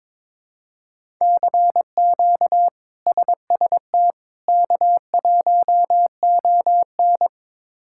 Morse code, heard as a keyed tone, about 22 wpm, 700 Hz. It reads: CQ SST K1ON
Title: K1USN SST QSO Practice - 22wpm